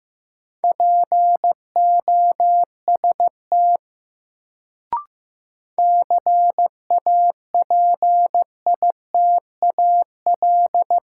Morse code, heard as a keyed tone, about 15 wpm, 700 Hz. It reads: POST CAPITAL
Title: Sentences from Top 1000 Words - New Word Drill - 15wpm